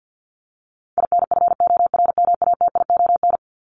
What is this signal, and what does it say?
Morse code, heard as a keyed tone, about 50 wpm, 700 Hz. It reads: INFORMATION
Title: Top 501-750 Words 50wpm